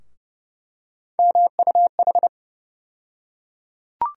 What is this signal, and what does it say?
Morse code, heard as a keyed tone, about 30 wpm, 700 Hz. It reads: MUH
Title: USA Call Sign Postfixes - 3 Letters 30wpm